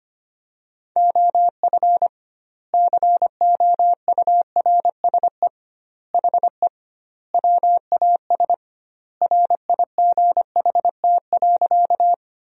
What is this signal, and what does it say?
Morse code, heard as a keyed tone, about 25 wpm, 700 Hz. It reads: OF COURSE HE WAS RIGHT.
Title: 2x Word-Spacing - Sentences from Top 500 Words - 25wpm